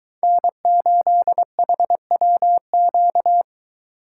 Morse code, heard as a keyed tone, about 23 wpm, 700 Hz. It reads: N8HWQ